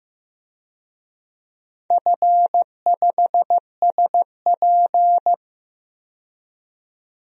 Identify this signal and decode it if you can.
Morse code, heard as a keyed tone, about 15 wpm, 700 Hz. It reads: F5SP
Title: DXCC-100 Call Signs - 4 Characters - 15wpm